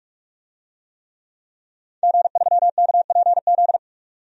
Morse code, heard as a keyed tone, about 46 wpm, 700 Hz. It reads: G3KPZ
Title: DXCC-100 Call Signs - 5 Characters - 45wpm